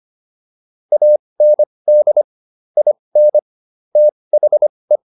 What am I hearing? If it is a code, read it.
Morse code, heard as a keyed tone, about 25 wpm, spaced as for 20 wpm, 600 Hz. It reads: AND IN THE